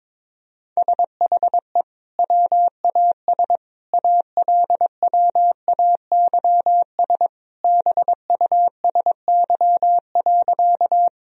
Morse code, heard as a keyed tone, about 22 wpm, 700 Hz. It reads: SHE WAS ALWAYS BUSY.
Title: Sentences from Top 1000 Words - Review Part 3 - 22wpm